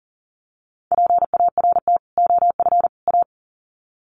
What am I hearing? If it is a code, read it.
Morse code, heard as a keyed tone, about 40 wpm, 700 Hz. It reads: PART OF A
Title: Sets of 3 Words 40wpm